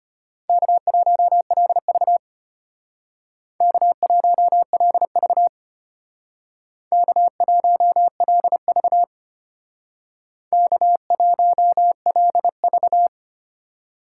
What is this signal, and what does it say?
Morse code, heard as a keyed tone, about 32 wpm, 700 Hz. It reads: K1LV K1LV K1LV K1LV